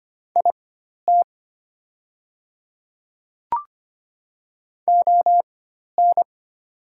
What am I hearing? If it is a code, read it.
Morse code, heard as a keyed tone, about 25 wpm, spaced as for 12 wpm, 700 Hz. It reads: IT ON
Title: 4x Character Spacing - Top 200 Words - 25wpm